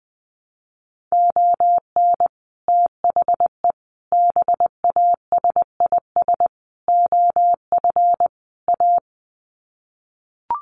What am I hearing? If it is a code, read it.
Morse code, heard as a keyed tone, about 20 wpm, 700 Hz. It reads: ON THE BASIS OF A
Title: Sets of 5 Words Encore 20wpm